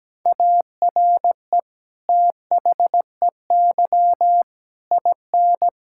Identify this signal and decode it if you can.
Morse code, heard as a keyed tone, about 17 wpm, 700 Hz. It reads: ARE THEY IN
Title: Sentences from Top 1000 Words - Review Part 5 - 17wpm